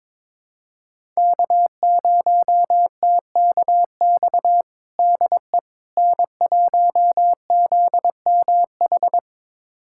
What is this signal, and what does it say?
Morse code, heard as a keyed tone, about 22 wpm, 700 Hz. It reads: K0TKX DE N1ZMH
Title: USA Callsign DE Callsign - 5 Characters - 22wpm